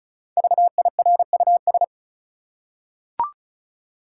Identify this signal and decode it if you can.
Morse code, heard as a keyed tone, about 36 wpm, 700 Hz. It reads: VIRUS
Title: Five Letter Words - 35wpm